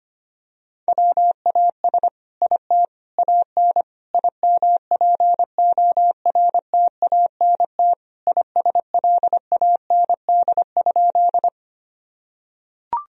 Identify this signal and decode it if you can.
Morse code, heard as a keyed tone, about 25 wpm, 700 Hz. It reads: WAS IT AN IMPORTANT ISLAND?
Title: Sentences from Top 1000 Words - Review Part 3 - 25wpm